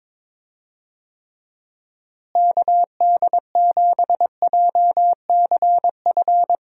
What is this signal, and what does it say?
Morse code, heard as a keyed tone, about 22 wpm, 700 Hz. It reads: KD7JCF
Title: USA Call Signs - 6 Characters - Repeated 2x - 22wpm